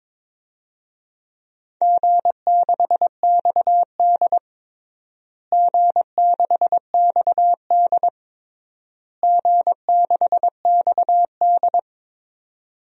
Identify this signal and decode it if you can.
Morse code, heard as a keyed tone, about 22 wpm, 700 Hz. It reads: G6XD G6XD G6XD
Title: DXCC-100 Call Signs - 4 Characters - Repeated 3x - 22wpm